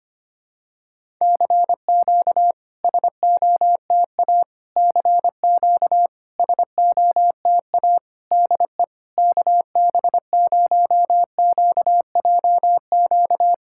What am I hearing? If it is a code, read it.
Morse code, heard as a keyed tone, about 25 wpm, 700 Hz. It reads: CQ SOTA CQ SOTA DE KB0QJQ
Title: SOTA QSO Practice 25wpm